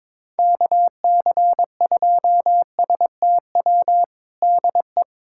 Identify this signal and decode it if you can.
Morse code, heard as a keyed tone, about 22 wpm, 700 Hz. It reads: KC2STW DE
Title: USA Callsign DE Callsign - 6 Characters - 22wpm